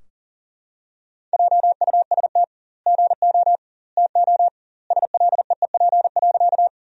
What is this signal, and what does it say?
Morse code, heard as a keyed tone, about 40 wpm, 700 Hz. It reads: JUST GO TO SLEEP.
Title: 2x Word-Spacing - Sentences from Top 500 Words - 40wpm